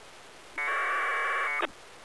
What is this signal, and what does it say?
Packet radio (AX.25, as in APRS).